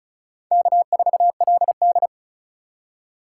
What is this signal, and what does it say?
Morse code, heard as a keyed tone, about 34 wpm, 690 Hz. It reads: K4LD
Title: USA Call Signs - 3x Character Spacing - 35wpm